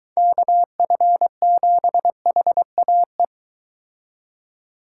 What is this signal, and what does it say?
Morse code, heard as a keyed tone, about 23 wpm, 700 Hz. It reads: KF7HAE